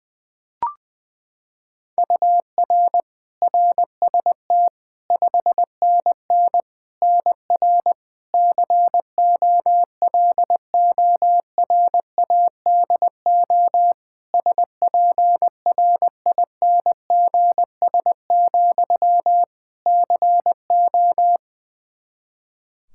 Morse code, heard as a keyed tone, about 20 wpm, 700 Hz. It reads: UR RST 5NN NR COLORADO SPRINGS, CO
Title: Practice QSO - Exchange 1 - Beginning - 20wpm